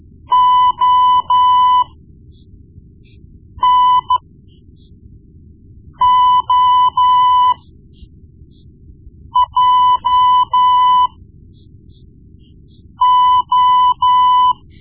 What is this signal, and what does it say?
Morse code, heard as a keyed tone, about 13 wpm, spaced as for 4 wpm, 1000 Hz. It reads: ONOJO
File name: Morse Lessons / Morse Lessons / Tape-1B